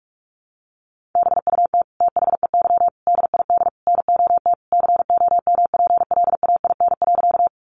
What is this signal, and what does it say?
Morse code, heard as a keyed tone, about 44 wpm, 700 Hz. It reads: BUT THEY DID NOT COMPLAIN.
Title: Sentences from Top 1300 Words 45wpm